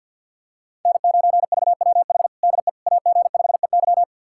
Morse code, heard as a keyed tone, about 53 wpm, 700 Hz. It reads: N9VWH DE AG5EY
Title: USA Callsign DE Callsign - 5 Characters - 50wpm